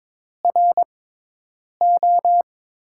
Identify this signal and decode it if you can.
Morse code, heard as a keyed tone, about 22 wpm, 700 Hz. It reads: R O
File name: Top 90 QSO Elements - 6x Character Spacing - 22wpm